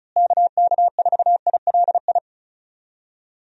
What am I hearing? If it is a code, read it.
Morse code, heard as a keyed tone, about 36 wpm, 700 Hz. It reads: KK4ILI
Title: USA Call Signs - 6 Characters - Repeated 3x - 35wpm